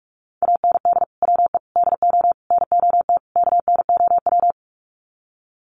Morse code, heard as a keyed tone, about 44 wpm, 700 Hz. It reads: AND WE DO NOT KNOW
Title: Sets of 5 Words Encore 45wpm